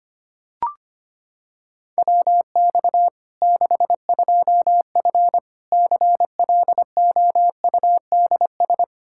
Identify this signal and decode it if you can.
Morse code, heard as a keyed tone, about 25 wpm, 700 Hz. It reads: WX 62F CLOUDS
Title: K1USN SST QSO Practice 25wpm